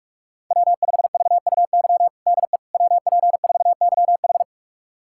Morse code, heard as a keyed tone, about 41 wpm, 700 Hz. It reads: W5VUY DE WP4YH
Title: USA Callsign DE Callsign - 5 Characters - 45wpm